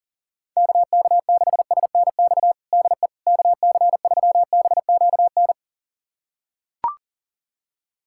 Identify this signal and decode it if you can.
Morse code, heard as a keyed tone, about 40 wpm, 700 Hz. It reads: KK6SNX DE KC3BQN
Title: USA Callsign DE Callsign - 6 Characters - 40wpm